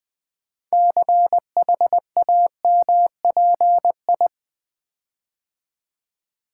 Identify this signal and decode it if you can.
Morse code, heard as a keyed tone, about 20 wpm, 700 Hz. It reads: CHAMPI
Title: Build A Word - 6 Letters At A Time - 12-Letter Words - 20wpm